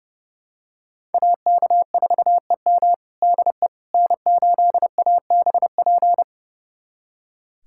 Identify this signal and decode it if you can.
Morse code, heard as a keyed tone, about 30 wpm, 700 Hz. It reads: AK4EM DE N8ABP
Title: USA Callsign DE Callsign - 5 Characters - 30wpm